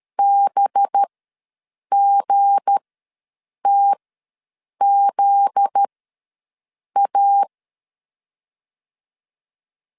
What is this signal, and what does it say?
Morse code, heard as a keyed tone, about 13 wpm, spaced as for 7 wpm, 800 Hz. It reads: BGTZA
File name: LEZIONI / 7 GF9Z / 7 GF9Z ws 12wpm 3321